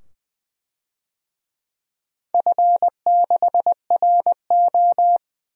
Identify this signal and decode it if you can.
Morse code, heard as a keyed tone, about 20 wpm, 700 Hz. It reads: F6RO
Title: DXCC-100 Call Signs - 4 Characters - 20wpm